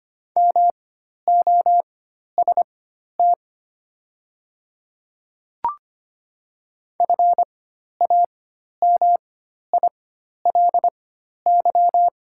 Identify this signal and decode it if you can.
Morse code, heard as a keyed tone, about 25 wpm, spaced as for 12 wpm, 690 Hz. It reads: MOST FAMILY
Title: 4x Character Spacing - Top 500 Words - 25wpm